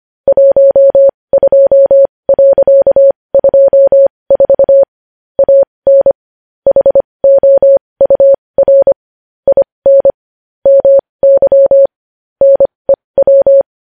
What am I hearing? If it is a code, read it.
Morse code, heard as a keyed tone, about 25 wpm, spaced as for 20 wpm, 550 Hz. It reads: 12.24 AN HOUR IN MY NEW
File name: podcast / News headlines-1x-25x20